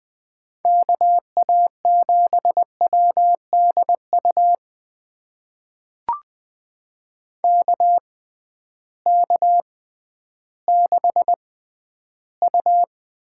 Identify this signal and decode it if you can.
Morse code, heard as a keyed tone, about 20 wpm, spaced as for 12 wpm, 700 Hz. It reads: KA7WDU KK6U
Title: USA Call Signs - 6 Characters - 6x Character Spacing - 20wpm